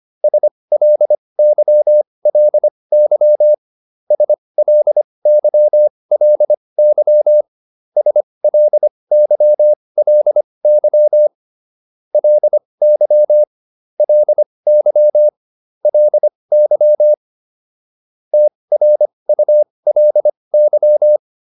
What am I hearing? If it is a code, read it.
Morse code, heard as a keyed tone, about 25 wpm, spaced as for 20 wpm, 600 Hz. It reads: SLYLY SLYLY SLYLY LY LY LY TRULY